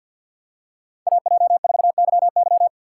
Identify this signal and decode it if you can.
Morse code, heard as a keyed tone, about 52 wpm, 700 Hz. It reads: AJ4YY
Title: USA Call Signs - 5 Characters - 6x Character Spacing - 50wpm